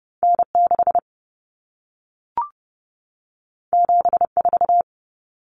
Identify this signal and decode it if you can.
Morse code, heard as a keyed tone, about 30 wpm, 700 Hz. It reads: N6 74
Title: Common Cut Numbers - 2-Digit 30wpm